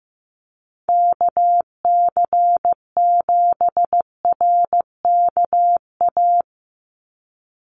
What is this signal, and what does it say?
Morse code, heard as a keyed tone, about 15 wpm, 700 Hz. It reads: KC7RKA
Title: USA Call Signs 15wpm